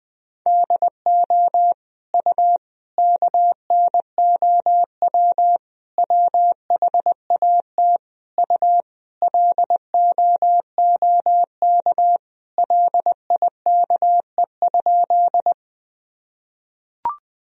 Morse code, heard as a keyed tone, about 20 wpm, 700 Hz. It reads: DO U KNOW WHAT U LOOK LIKE?
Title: Sentences from Top 100 Words and Abbreviations 20wpm